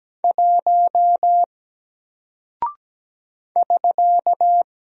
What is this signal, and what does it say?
Morse code, heard as a keyed tone, about 17 wpm, 700 Hz. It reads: 1 <SK>
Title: Single Character (Letter:Number:Punctuation:Prosign) Alt 17wpm